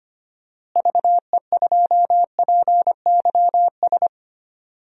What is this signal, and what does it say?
Morse code, heard as a keyed tone, about 25 wpm, 700 Hz. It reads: VE2PYS
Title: DXCC-100 Call Signs - 6 Characters - Repeated 3x - 25wpm